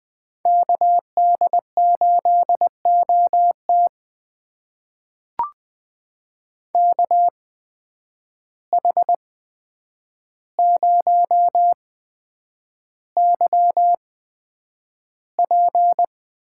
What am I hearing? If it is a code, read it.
Morse code, heard as a keyed tone, about 20 wpm, spaced as for 9 wpm, 700 Hz. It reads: KD8OT KH0YP
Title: USA Call Signs - 5 Characters - 8x Character Spacing - 20wpm